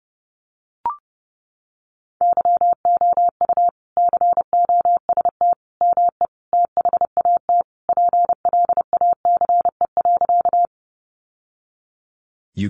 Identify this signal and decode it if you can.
Morse code, heard as a keyed tone, about 28 wpm, 700 Hz. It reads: YOU COST ME THAT PLACE.
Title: Sentences from Top 800 Words 30wpm